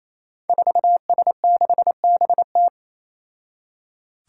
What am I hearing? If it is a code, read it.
Morse code, heard as a keyed tone, about 28 wpm, 700 Hz. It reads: 4S6BT
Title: Five Challenging Letter-Number Combinations 28wpm